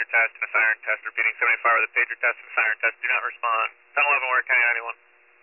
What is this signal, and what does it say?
Emergency Alert System (SAME) tones (header bursts), heard as harsh buzzing bursts, with speech. It is heard from 0.5 to 4.8 s.